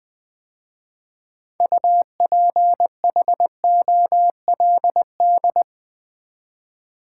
Morse code, heard as a keyed tone, about 20 wpm, 700 Hz. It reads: UPHOLD
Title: 6 Letter Words - First 3 Letters Given - 20wpm